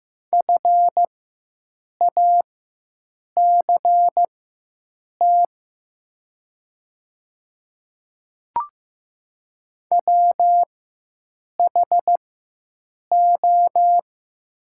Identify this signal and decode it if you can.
Morse code, heard as a keyed tone, about 15 wpm, spaced as for 7 wpm, 700 Hz. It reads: FACT WHO
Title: 4x Character Spacing - Top 500 Words - 15wpm